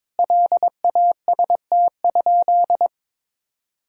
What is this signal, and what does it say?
Morse code, heard as a keyed tone, about 22 wpm, 700 Hz. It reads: LAST?
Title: Sentences from Top 1000 Words - Review Part 4 - 22wpm